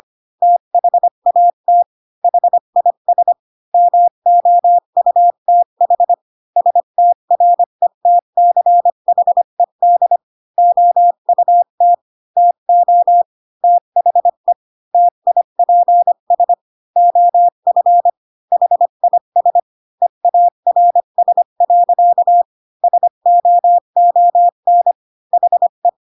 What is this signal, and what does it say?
Morse code, heard as a keyed tone, about 25 wpm, 700 Hz. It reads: THAT HIS MOUTH STRETCHED OUT TO THE TIPS OF HIS EARS. SOON HE
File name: Pinocchio / Pinocchio ch5 20 25 ik1hge